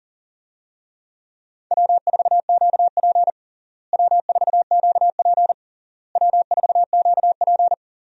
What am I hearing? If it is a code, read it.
Morse code, heard as a keyed tone, about 40 wpm, 700 Hz. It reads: W4QP W4QP W4QP
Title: USA Call Signs - 4 Characters - Repeated 3x - 40wpm